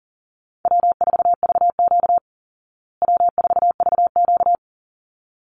Morse code, heard as a keyed tone, about 40 wpm, 700 Hz. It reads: W4VQ W4VQ
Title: Easy Call Signs Repeated 2x - 40wpm